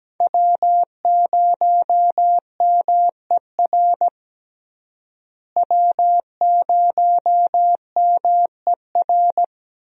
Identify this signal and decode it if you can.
Morse code, heard as a keyed tone, about 17 wpm, 700 Hz. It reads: W0MER W0MER
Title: USA Call Signs - 5 Characters - Repeated 3x - 17wpm